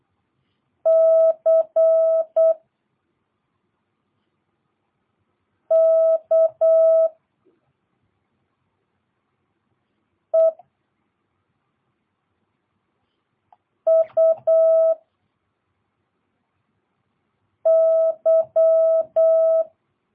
Morse code, heard as a keyed tone, about 8 wpm, 650 Hz. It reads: C K E U Y